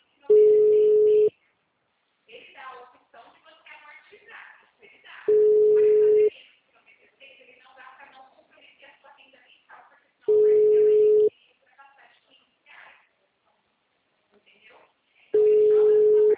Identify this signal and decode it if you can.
Ringback tone (Europe, 425 Hz), with speech.